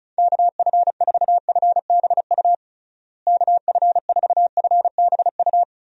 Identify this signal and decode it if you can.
Morse code, heard as a keyed tone, about 36 wpm, 700 Hz. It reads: KF4FBU KF4FBU
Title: USA Call Signs - 6 Characters - Repeated 2x - 35wpm